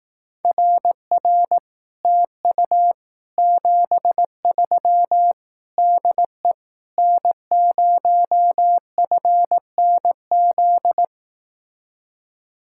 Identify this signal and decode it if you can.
Morse code, heard as a keyed tone, about 18 wpm, 700 Hz. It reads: RR TU 73 DE N0FNZ
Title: Speed-Racing Common QSO Phrases Encore - 15wpm